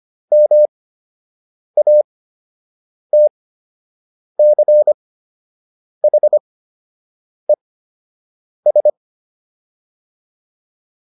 Morse code, heard as a keyed tone, about 25 wpm, spaced as for 7 wpm, 600 Hz. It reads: MATCHES